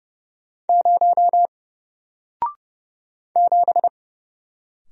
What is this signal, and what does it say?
Morse code, heard as a keyed tone, about 30 wpm, 700 Hz. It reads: 0 7
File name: Single-Letter-Number-30wpm